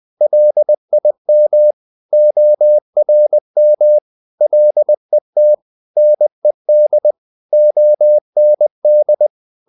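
Morse code, heard as a keyed tone, about 20 wpm, 600 Hz. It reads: LIM ORM LET NED OND
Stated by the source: Morse code practice of short Danish words at 20 wpm, from CW med Gnister.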